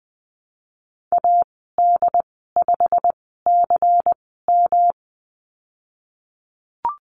Morse code, heard as a keyed tone, about 20 wpm, spaced as for 15 wpm, 700 Hz. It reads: AD5CM
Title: 154 Lesson 25g - Letter p - Call Signs 20wpm